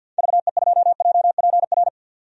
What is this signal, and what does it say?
Morse code, heard as a keyed tone, about 46 wpm, 700 Hz. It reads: VE2JPR
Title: DXCC-100 Call Signs - 6 Characters - 50wpm